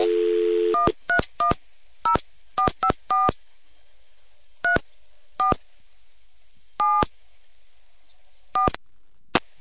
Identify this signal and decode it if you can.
DTMF, heard as telephone touch tones, tones of about 120 ms. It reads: 1310454 34 7 4